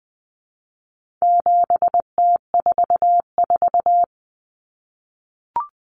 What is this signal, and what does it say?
Morse code, heard as a keyed tone, about 20 wpm, 700 Hz. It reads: 7T44
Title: Common Cut Numbers - 4-Digit 20wpm